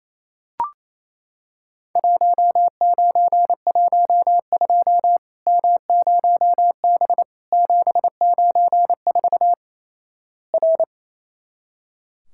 Morse code, heard as a keyed tone, about 28 wpm, 700 Hz. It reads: 1912 M0B 794 R
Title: Worked All Europe DX Contest - QTC Exchange - 28wpm